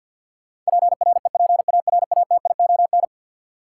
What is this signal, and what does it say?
Morse code, heard as a keyed tone, about 53 wpm, 700 Hz. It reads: PREPARATION
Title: Sentences from Top 1400 Words - New Word Drill - 50wpm